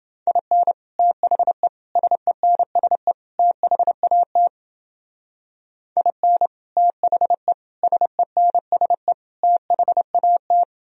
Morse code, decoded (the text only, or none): IN THE SENSE THAT IN THE SENSE THAT